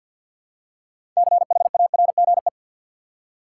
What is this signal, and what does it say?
Morse code, heard as a keyed tone, about 50 wpm, 700 Hz. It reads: CHARGE
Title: Six Letter Words - 50wpm